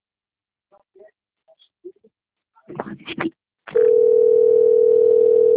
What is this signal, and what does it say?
Ringback tone (North America), 440 + 480 Hz.
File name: rg-1600-unknown- - - .38574